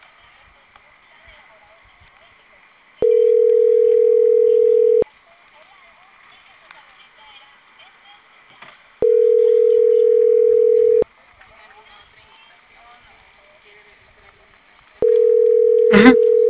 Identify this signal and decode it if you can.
Ringback tone (North America), 440 + 480 Hz, with speech.
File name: CENTRAL CAEA / out agcaea02 - .10045